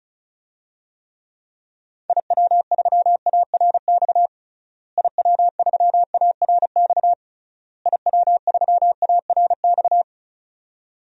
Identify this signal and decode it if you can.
Morse code, heard as a keyed tone, about 34 wpm, 700 Hz. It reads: IW3ARX IW3ARX IW3ARX
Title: DXCC-100 Call Signs - 6 Characters - Repeated 3x - 35wpm